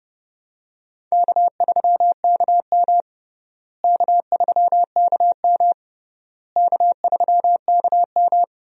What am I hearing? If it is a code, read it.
Morse code, heard as a keyed tone, about 32 wpm, 700 Hz. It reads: K3KM K3KM K3KM
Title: USA Call Signs - 4 Characters - Repeated 3x - 30wpm